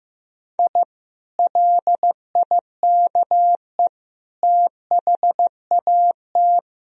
Morse code, heard as a keyed tone, about 15 wpm, 700 Hz. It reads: I LIKE THAT
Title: Sentences from Top 1000 Words - Review Part 3 - 15wpm